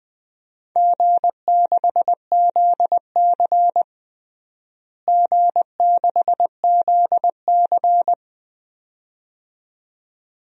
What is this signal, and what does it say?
Morse code, heard as a keyed tone, about 21 wpm, 700 Hz. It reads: G6ZC G6ZC
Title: DXCC-100 Call Signs - 4 Characters - Repeated 2x - 20wpm